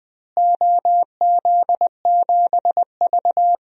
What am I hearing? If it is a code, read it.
Morse code, heard as a keyed tone, about 20 wpm, 700 Hz. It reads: OZ7V
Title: DXCC-100 Call Signs - 4 Characters - Repeated 3x - 20wpm